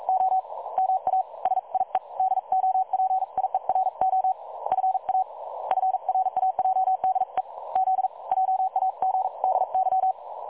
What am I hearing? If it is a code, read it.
Morse code, heard as a keyed tone, about 43 wpm, spaced as for 36 wpm, 760 Hz. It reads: O MA SIE DOPIERO FAFRAJDE Z JAGDY
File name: dlazaawansowanych / nagraniazpasma / websdr recording 3565.0kHz